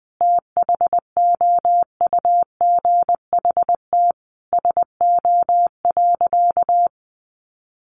Morse code, heard as a keyed tone, about 20 wpm, 700 Hz. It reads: THOUGHT SO.
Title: Sentences from Top 1400 Words 20wpm